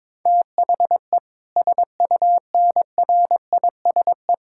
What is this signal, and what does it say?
Morse code, heard as a keyed tone, about 22 wpm, 700 Hz. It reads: THE SUNRISE
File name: Difficult Sentences with Hints - 22wpm